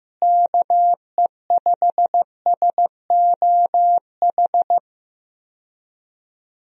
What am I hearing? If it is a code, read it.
Morse code, heard as a keyed tone, about 15 wpm, 700 Hz. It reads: KE5SOH
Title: USA Call Signs - 6 Characters - Repeated 3x - 15wpm